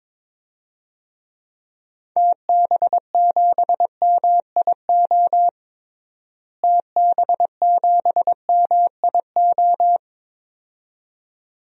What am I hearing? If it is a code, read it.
Morse code, heard as a keyed tone, about 22 wpm, 700 Hz. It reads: TB7MIO TB7MIO
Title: DXCC-100 Call Signs - 6 Characters - Repeated 2x - 22wpm